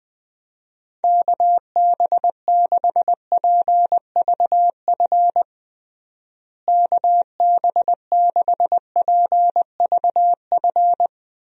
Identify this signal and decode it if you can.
Morse code, heard as a keyed tone, about 21 wpm, 700 Hz. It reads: KB6PVF KB6PVF
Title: USA Call Signs - 6 Characters - Repeated 2x - 20wpm